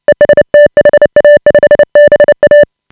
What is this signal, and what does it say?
Morse code, heard as a keyed tone, about 29 wpm, 590 Hz. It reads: EST HA5BA
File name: AUDIO2018 / HA5BA CW